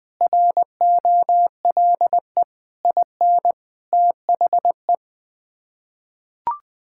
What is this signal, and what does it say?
Morse code, heard as a keyed tone, about 21 wpm, 700 Hz. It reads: ROLE IN THE
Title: Sets of 6 Words - 20wpm